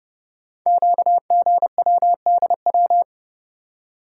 Morse code, heard as a keyed tone, about 30 wpm, 700 Hz. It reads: QGWDW
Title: Code Groups - 5 Letters - 30wpm